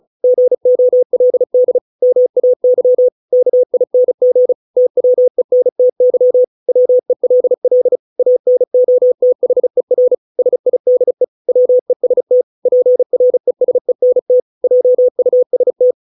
Morse code, heard as a keyed tone, about 34 wpm, 500 Hz. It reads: GOLD MAY KING TWENTY WELL ANOTHER SIDE WEST PRESENT JUST